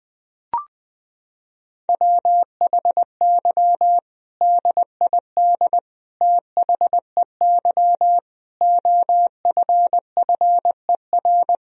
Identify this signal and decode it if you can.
Morse code, heard as a keyed tone, about 20 wpm, 700 Hz. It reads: WHY DID THEY OFFER
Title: Sentences from Top 1000 Words - Review Part 4 - 20wpm